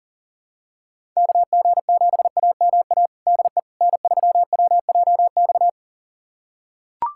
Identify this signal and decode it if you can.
Morse code, heard as a keyed tone, about 40 wpm, 700 Hz. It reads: KG7AMA DE N3WJX
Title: USA Callsign DE Callsign 40wpm